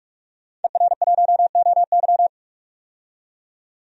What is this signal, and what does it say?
Morse code, heard as a keyed tone, about 46 wpm, 700 Hz. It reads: ER1OY
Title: DXCC-100 Call Signs - 5 Characters - 45wpm